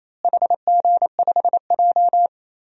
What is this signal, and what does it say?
Morse code, heard as a keyed tone, about 28 wpm, 700 Hz. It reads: HG5J
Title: DXCC-100 Call Signs - 4 Characters - Repeated 3x - 28wpm